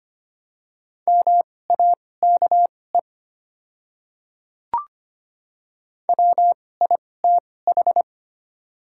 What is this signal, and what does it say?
Morse code, heard as a keyed tone, about 25 wpm, spaced as for 18 wpm, 690 Hz. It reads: MAKE WITH
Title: 2x Character Spacing - Top 100 Words - 25wpm